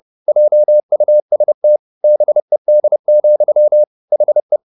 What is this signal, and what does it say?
Morse code, heard as a keyed tone, about 30 wpm, 600 Hz. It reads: JUST BED, HE